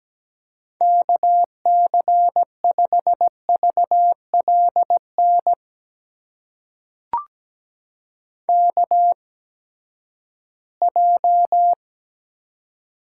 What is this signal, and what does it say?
Morse code, heard as a keyed tone, about 17 wpm, spaced as for 12 wpm, 700 Hz. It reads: KC5VLN KJ
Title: USA Call Signs - 6 Characters - 8x Character Spacing - 17wpm